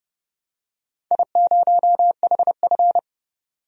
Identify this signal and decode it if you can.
Morse code, heard as a keyed tone, about 31 wpm, 700 Hz. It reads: I0HF
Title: DXCC-100 Call Signs - 4 Characters - 30wpm